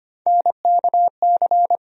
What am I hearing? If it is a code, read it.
Morse code, heard as a keyed tone, about 25 wpm, 700 Hz. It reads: NKC
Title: USA Call Sign Postfixes - 3 Letters 25wpm